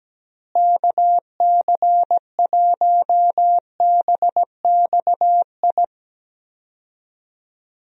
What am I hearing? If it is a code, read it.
Morse code, heard as a keyed tone, about 17 wpm, 700 Hz. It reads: KC1BXI
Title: USA Call Signs - 6 Characters - Repeated 2x - 17wpm